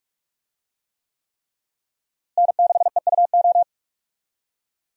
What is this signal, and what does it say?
Morse code, heard as a keyed tone, about 46 wpm, 700 Hz. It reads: N6EUO
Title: USA Call Signs - 5 Characters - Repeated 3x - 45wpm